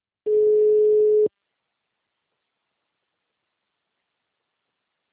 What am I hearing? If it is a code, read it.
Ringback tone (Europe, 425 Hz).